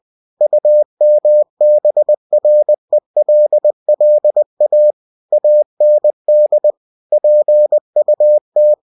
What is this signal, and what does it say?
Morse code, heard as a keyed tone, about 20 wpm, 600 Hz. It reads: UMBRELLA AND PUT